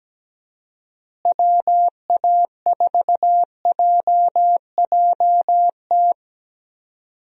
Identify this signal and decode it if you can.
Morse code, heard as a keyed tone, about 17 wpm, 700 Hz. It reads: WA4JJT
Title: USA Call Signs - 6 Characters - Repeated 2x - 17wpm